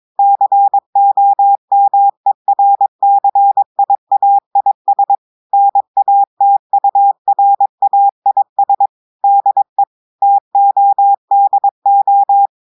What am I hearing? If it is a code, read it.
Morse code, heard as a keyed tone, about 22 wpm, 800 Hz. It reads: COMERCIAIS NATURAIS DE TODO
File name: audios / portugais / portugais texte en clair 2 vitesse 22 mots minutes